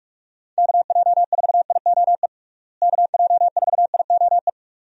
Morse code, heard as a keyed tone, about 46 wpm, 700 Hz. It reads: KJ4IOE KJ4IOE
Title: USA Call Signs - 6 Characters - Repeated 2x - 45wpm